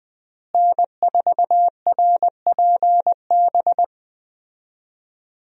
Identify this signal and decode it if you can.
Morse code, heard as a keyed tone, about 20 wpm, 700 Hz. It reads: N4RPB
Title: USA Call Signs - 5 Characters - Repeated 3x - 20wpm